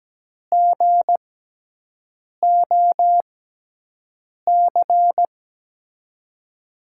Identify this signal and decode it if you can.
Morse code, heard as a keyed tone, about 17 wpm, spaced as for 6 wpm, 700 Hz. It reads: GOC
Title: USA Call Signs - 6x Character Spacing - 17wpm